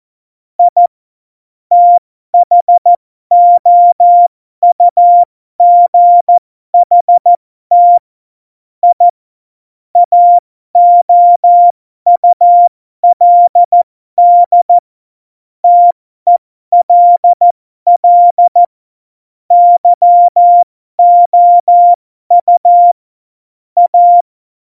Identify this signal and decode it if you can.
Morse code, heard as a keyed tone, about 14 wpm, spaced as for 12 wpm, 700 Hz. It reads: I THOUGHT I AOULD TELL YOU A
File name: Morsekurs / Witze